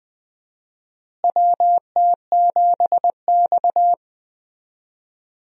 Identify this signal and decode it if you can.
Morse code, heard as a keyed tone, about 20 wpm, 700 Hz. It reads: WT7X
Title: USA Call Signs - 4 Characters - Repeated 3x - 20wpm